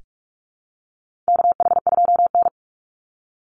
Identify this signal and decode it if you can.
Morse code, heard as a keyed tone, about 44 wpm, 700 Hz. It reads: KH2N
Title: Call Signs - 45wpm